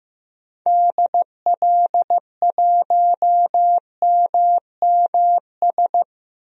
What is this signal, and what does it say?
Morse code, heard as a keyed tone, about 15 wpm, 700 Hz. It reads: DL1MMS
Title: DXCC-100 Call Signs - 6 Characters - 15wpm